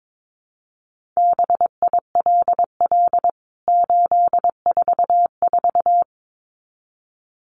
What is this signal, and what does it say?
Morse code, heard as a keyed tone, about 22 wpm, 700 Hz. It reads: BILL 844
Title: CWT Exchange 22wpm